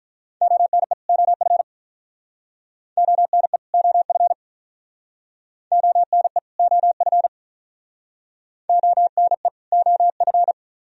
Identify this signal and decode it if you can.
Morse code, heard as a keyed tone, about 47 wpm, 700 Hz. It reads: ONE OF ONE OF ONE OF ONE OF